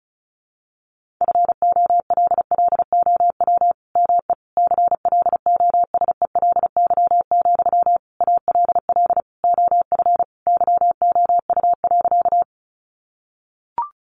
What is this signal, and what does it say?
Morse code, heard as a keyed tone, about 34 wpm, 700 Hz. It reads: FOLLOW ME CLOSELY, ALL OF YOU.
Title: Sentences from Top 1400 Words 35wpm